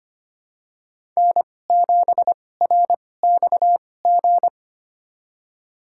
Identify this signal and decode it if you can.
Morse code, heard as a keyed tone, about 25 wpm, spaced as for 18 wpm, 700 Hz. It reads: N7RXG
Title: USA Call Signs - 5 Characters - 2x Character Spacing - 25wpm